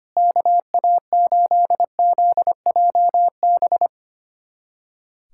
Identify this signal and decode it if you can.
Morse code, heard as a keyed tone, about 25 wpm, 700 Hz. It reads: KA8ZJB
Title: USA Callsign DE Callsign - 6 Characters - 25wpm